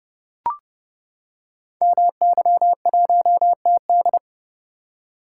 Morse code, heard as a keyed tone, about 30 wpm, 700 Hz. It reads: MY1TD
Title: Code Groups - 5 Letter:Numbers - 30wpm